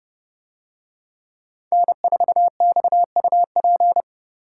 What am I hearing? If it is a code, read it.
Morse code, heard as a keyed tone, about 29 wpm, 700 Hz. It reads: N4XUP